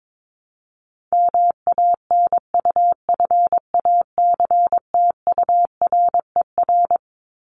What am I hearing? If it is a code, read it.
Morse code, heard as a keyed tone, about 22 wpm, 700 Hz. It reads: MANUFACTURER
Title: Top 1751-2000 Words 22wpm